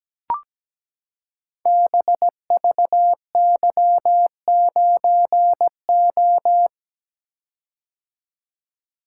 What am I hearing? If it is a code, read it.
Morse code, heard as a keyed tone, about 17 wpm, 700 Hz. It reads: BVY9O
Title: Code Groups - 5 Letter:Numbers - 17wpm